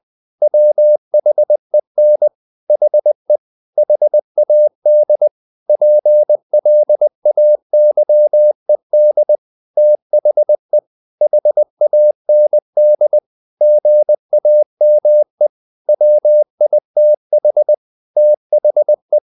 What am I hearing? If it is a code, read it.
Morse code, heard as a keyed tone, about 20 wpm, 600 Hz. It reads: WHEN HE HAD PLAYED THE HAND GAME WITH THE
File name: the old man and the sea / the old man and the sea part 1 20wpm